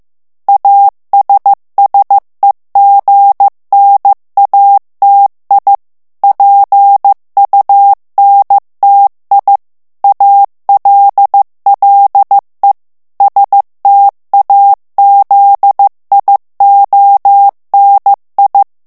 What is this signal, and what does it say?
Morse code, heard as a keyed tone, about 15 wpm, 800 Hz. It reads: ASSEGNATI PUNTI ALLE STAZIONI
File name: morse mp3 / ITALIEN texte clair 2 vitesse 14 mots minute